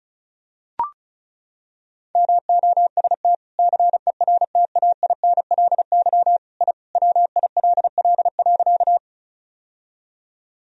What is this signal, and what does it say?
Morse code, heard as a keyed tone, about 36 wpm, 700 Hz. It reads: MOST CERTAINLY I WILL.
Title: Sentences from Top 1000 Words - Review Part 1 - 35wpm